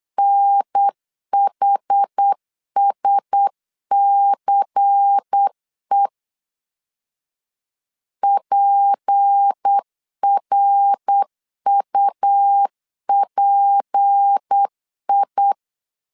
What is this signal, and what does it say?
Morse code, heard as a keyed tone, about 8 wpm, 800 Hz. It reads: NHSCE PRUPI
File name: LEZIONI / 6 PKX0 / 6 P 8wpm 3025